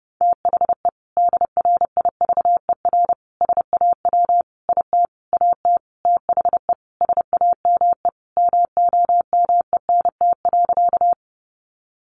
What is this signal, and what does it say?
Morse code, heard as a keyed tone, about 30 wpm, 700 Hz. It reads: THE DRIVER SAW IT AT THE SAME MOMENT.
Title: Sentences from Top 1000 Words 30wpm